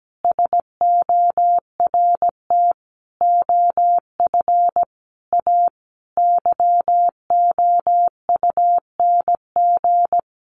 Morse code, heard as a keyed tone, about 17 wpm, 700 Hz. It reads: SORT OF A YOUNG
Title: Sentences from Top 1300 Words 17wpm